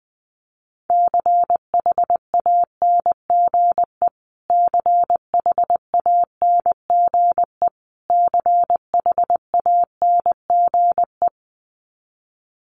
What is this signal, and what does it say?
Morse code, heard as a keyed tone, about 20 wpm, 700 Hz. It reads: CHANGE CHANGE CHANGE
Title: Top 200 Words Repeated 20wpm